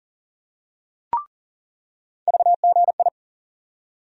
Morse code, heard as a keyed tone, about 40 wpm, 700 Hz. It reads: VGI
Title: USA Call Sign Postfixes - 3 Letters 40wpm